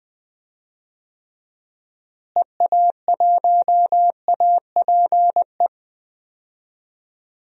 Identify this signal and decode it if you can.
Morse code, heard as a keyed tone, about 21 wpm, 700 Hz. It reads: EA1APE
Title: DXCC-100 Call Signs - 6 Characters - Repeated 2x - 20wpm